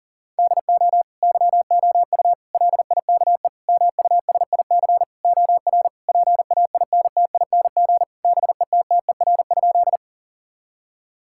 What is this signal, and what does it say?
Morse code, heard as a keyed tone, about 40 wpm, 700 Hz. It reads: DO YOU LIKE MUSIC OR PAINTING BETTER?
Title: Sentences from Top 1000 Words - Review Part 6 - 40wpm